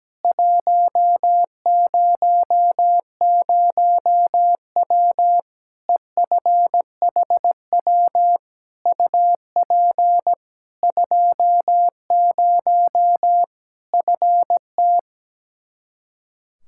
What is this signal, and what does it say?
Morse code, heard as a keyed tone, about 17 wpm, 700 Hz. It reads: 100W EFHW UP 20 FT
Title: K1USN SST QSO Practice 17wpm